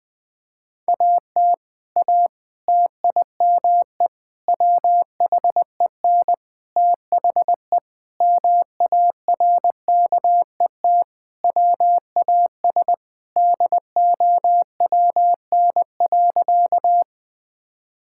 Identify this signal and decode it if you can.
Morse code, heard as a keyed tone, about 20 wpm, 700 Hz. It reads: AT A TIME WHEN THE MARKET WAS DOWN.
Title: Sentences with 5-Word Phrases - Spoken Beginning - 20wpm